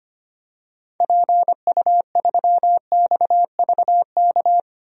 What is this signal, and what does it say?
Morse code, heard as a keyed tone, about 25 wpm, 700 Hz. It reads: PU3XVK
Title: DXCC-100 Call Signs - 6 Characters - 25wpm